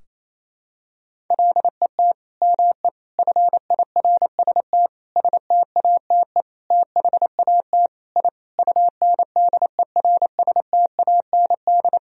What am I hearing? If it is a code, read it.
Morse code, heard as a keyed tone, about 28 wpm, 700 Hz. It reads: LET ME FIRST STATE THAT I UNDERSTAND
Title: Sentences from Top 1000 Words - Review Part 2 - 28wpm